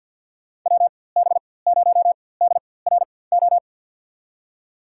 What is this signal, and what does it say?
Morse code, heard as a keyed tone, about 53 wpm, spaced as for 24 wpm, 700 Hz. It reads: WB0DRO
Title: USA Call Signs - 6 Characters - 4x Character Spacing - 50wpm